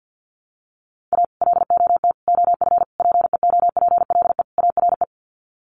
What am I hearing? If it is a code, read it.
Morse code, heard as a keyed tone, about 50 wpm, 700 Hz. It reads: A LOT OF PEOPLE ARE
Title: Sets of 5 Words 50wpm